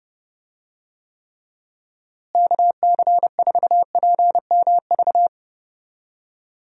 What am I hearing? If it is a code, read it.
Morse code, heard as a keyed tone, about 28 wpm, 700 Hz. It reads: KC4PMV
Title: USA Call Signs - 6 Characters - Repeated 3x - 30wpm